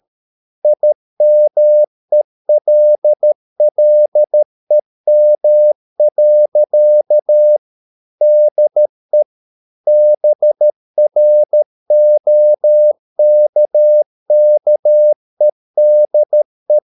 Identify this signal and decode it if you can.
Morse code, heard as a keyed tone, about 13 wpm, 600 Hz. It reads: IMELLEM. DE BROKKEDE
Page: Ferie 13 wpm